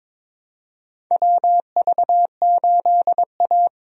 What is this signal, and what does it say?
Morse code, heard as a keyed tone, about 22 wpm, 700 Hz. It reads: WV8A
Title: USA Call Signs - 4 Characters - Repeated 3x - 22wpm